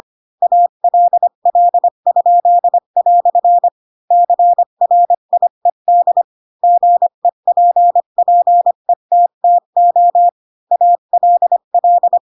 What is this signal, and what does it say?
Morse code, heard as a keyed tone, about 25 wpm, 700 Hz. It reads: ALL?" CRIED GEPPETTO ALL
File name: Pinocchio / Pinocchio ch8 20 25 ik1hge